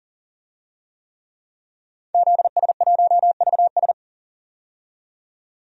Morse code, heard as a keyed tone, about 40 wpm, 700 Hz. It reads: ZS1VS
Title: DXCC-100 Call Signs - 5 Characters - 40wpm